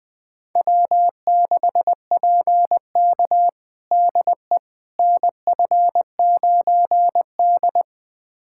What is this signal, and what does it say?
Morse code, heard as a keyed tone, about 20 wpm, 700 Hz. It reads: W6PK DE NF9D
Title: USA Callsign DE Callsign - 4 Characters - 20wpm